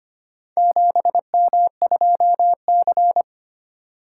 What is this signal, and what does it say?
Morse code, heard as a keyed tone, about 25 wpm, 700 Hz. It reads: 7M2C
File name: practice / 25wpm / Four Letter-Number Combinations - 25wpm